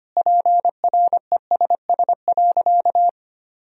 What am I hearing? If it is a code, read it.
Morse code, heard as a keyed tone, about 25 wpm, 700 Hz. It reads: PRESS.
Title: Sentences from Top 1000 Words - Review Part 1 - 25wpm